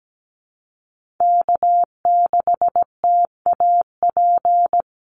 Morse code, heard as a keyed tone, about 17 wpm, 700 Hz. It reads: K6TAP
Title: USA Call Signs Repeated 2x 17wpm